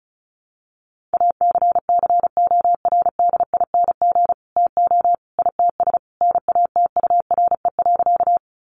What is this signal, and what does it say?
Morse code, heard as a keyed tone, about 34 wpm, 700 Hz. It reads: ACCORDING TO ITS NATURE.
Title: Sentences from Top 700 Words 35wpm